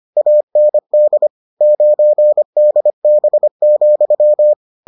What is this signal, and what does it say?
Morse code, heard as a keyed tone, about 25 wpm, 600 Hz. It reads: AND 9DB,